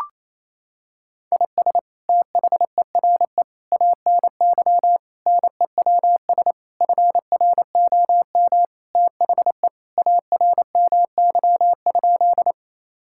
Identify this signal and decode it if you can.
Morse code, heard as a keyed tone, about 28 wpm, 700 Hz. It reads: IS THERE ANY NEWS FROM THE ARMY?
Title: Sentences from Top 1000 Words - Review Part 5 - 28wpm